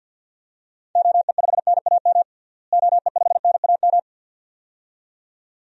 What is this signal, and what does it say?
Morse code, heard as a keyed tone, about 51 wpm, 700 Hz. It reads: OE5NAM OE5NAM
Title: DXCC-100 Call Signs - 6 Characters - Repeated 2x - 50wpm